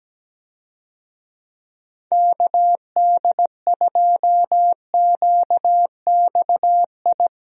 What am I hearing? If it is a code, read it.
Morse code, heard as a keyed tone, about 17 wpm, 700 Hz. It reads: KD2QXI
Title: USA Call Signs - 6 Characters - Repeated 3x - 17wpm